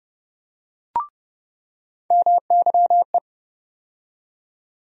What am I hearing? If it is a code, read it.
Morse code, heard as a keyed tone, about 31 wpm, 700 Hz. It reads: MYE
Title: USA Call Sign Postfixes - 3 Letters 30wpm